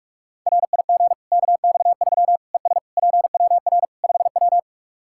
Morse code, heard as a keyed tone, about 41 wpm, 700 Hz. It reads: RIG KX3 ES PWR 5W
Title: Practice QSO - Exchange 2 - Info - 45wpm